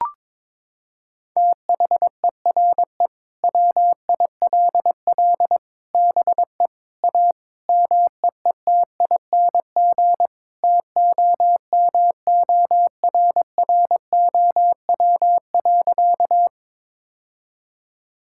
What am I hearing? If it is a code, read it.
Morse code, heard as a keyed tone, about 22 wpm, 700 Hz. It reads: THERE WILL BE A MEETING TOMORROW.
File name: practice / 22wpm / Sentences with 4-Word Phrases - 22wpm